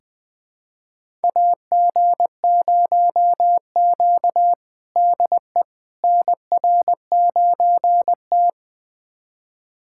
Morse code, heard as a keyed tone, about 20 wpm, 700 Hz. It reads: AG0Q DE NR9T
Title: USA Callsign DE Callsign - 4 Characters - 20wpm